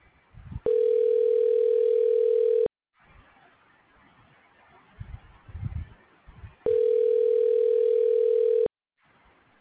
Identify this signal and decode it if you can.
Ringback tone (North America), 440 + 480 Hz.